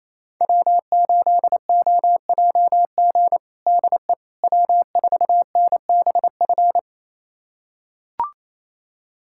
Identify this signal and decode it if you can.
Morse code, heard as a keyed tone, about 28 wpm, 700 Hz. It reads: W8OJG DE W4NBF
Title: USA Callsign DE Callsign - 5 Characters - 28wpm